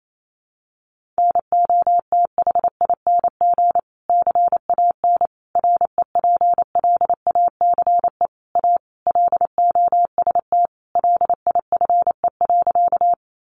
Morse code, heard as a keyed tone, about 28 wpm, 700 Hz. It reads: NOTHING CAN REPLACE A LOST LIFE.
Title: Sentences from Top 1400 Words 28wpm